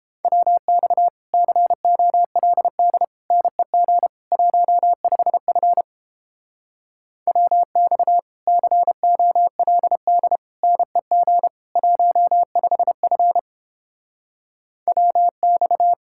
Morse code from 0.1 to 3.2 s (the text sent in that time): WX COLD